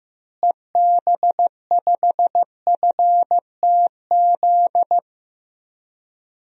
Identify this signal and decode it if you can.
Morse code, heard as a keyed tone, about 15 wpm, 700 Hz. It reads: EB5FTZ
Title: DXCC-100 Call Signs - 6 Characters - Repeated 2x - 15wpm